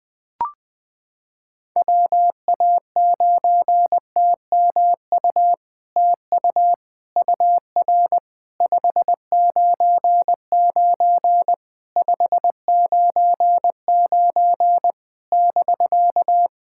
Morse code, heard as a keyed tone, about 20 wpm, 700 Hz. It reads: WA9TMU TU UR 599 599 <BK>
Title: POTA QSO Practice - 20wpm